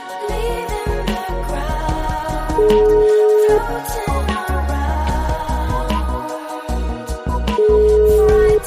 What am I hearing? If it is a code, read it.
Ringback tone (Europe, 425 Hz), with noise.